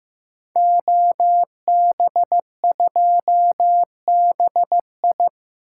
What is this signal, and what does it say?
Morse code, heard as a keyed tone, about 15 wpm, 700 Hz. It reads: OB2BI
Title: Code Groups - 5 Letter:Numbers - 15wpm